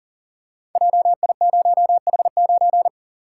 Morse code, heard as a keyed tone, about 40 wpm, 700 Hz. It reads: JI0H9
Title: Code Groups - 5 Letter:Numbers - 40wpm